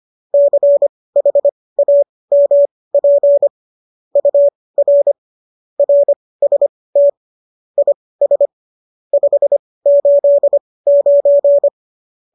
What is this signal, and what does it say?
Morse code, heard as a keyed tone, about 25 wpm, spaced as for 18 wpm, 570 Hz. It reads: CHAMP UR RST IS 589